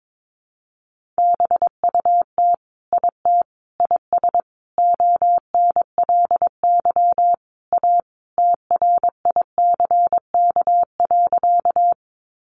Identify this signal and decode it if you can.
Morse code, heard as a keyed tone, about 22 wpm, 700 Hz. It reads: BUT IT IS ONLY A TRICK.
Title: Sentences from Top 1500 Words 22wpm